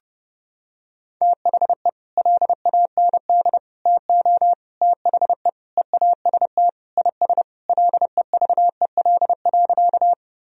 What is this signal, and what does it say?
Morse code, heard as a keyed tone, about 30 wpm, 700 Hz. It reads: THE LAND TO THE EAST IS LEVEL.
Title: Sentences from Top 1000 Words - Review Part 4 - 30wpm